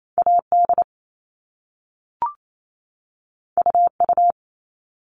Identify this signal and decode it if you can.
Morse code, heard as a keyed tone, about 28 wpm, 700 Hz. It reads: AD UU
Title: All Cut Numbers - 2-Digit 28wpm